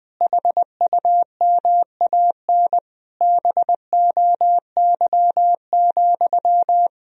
Morse code, heard as a keyed tone, about 20 wpm, 700 Hz. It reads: HUMAN BOY,
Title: Sentences from Top 1000 Words - Review Part 3 - 20wpm